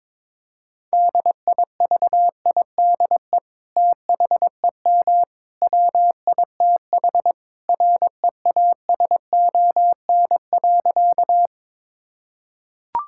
Morse code, heard as a keyed tone, about 21 wpm, 700 Hz. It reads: DIVIDE THEM WITH REASON.
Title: Sentences from Top 1600 Words 22wpm